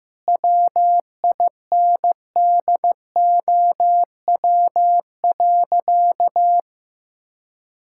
Morse code, heard as a keyed tone, about 15 wpm, 700 Hz. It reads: WINDOW.
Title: Sentences from Top 1600 Words 15wpm